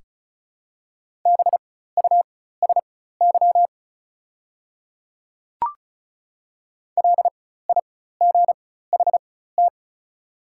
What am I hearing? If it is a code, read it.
Morse code, heard as a keyed tone, about 36 wpm, spaced as for 16 wpm, 690 Hz. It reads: BUSY LIGHT
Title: 4x Character Spacing - Top 500 Words - 35wpm